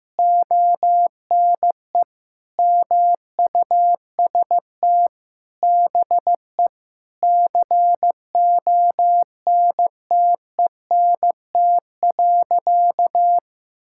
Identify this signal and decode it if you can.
Morse code, heard as a keyed tone, about 15 wpm, 700 Hz. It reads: ONE MUST BE CONTENT.
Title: Sentences from Top 1000 Words - Review Part 6 - 15wpm